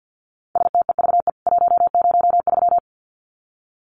Morse code, heard as a keyed tone, about 50 wpm, 700 Hz. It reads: STEVE 103
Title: CWT Exchange 50wpm